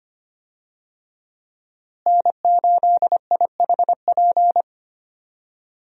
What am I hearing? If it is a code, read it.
Morse code, heard as a keyed tone, about 25 wpm, 700 Hz. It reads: N8IHP
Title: USA Call Signs - 5 Characters - Repeated 3x - 25wpm